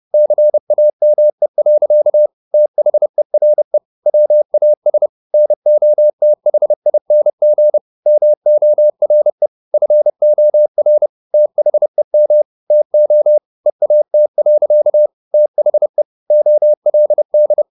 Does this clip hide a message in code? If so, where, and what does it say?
Yes: Morse code from 0.1 to 17.6 s, heard as a keyed tone, about 30 wpm, 600 Hz. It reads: CAME. THERE WAS NOTHING MORE FOR THEM TO EAT. THE OLD